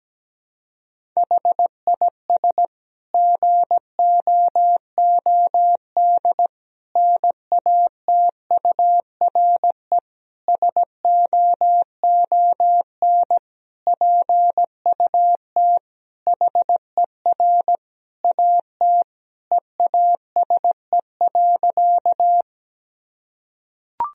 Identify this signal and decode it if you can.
Morse code, heard as a keyed tone, about 17 wpm, 700 Hz. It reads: HIS GOOD NATURE SOON PUT HER AT EASE.
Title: Sentences from Top 1000 Words - Review Part 1 - 17wpm